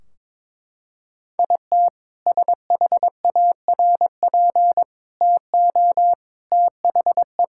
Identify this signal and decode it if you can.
Morse code, heard as a keyed tone, about 22 wpm, 700 Hz. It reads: IT SHARP TO THE
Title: Sentences from Top 1600 Words 22wpm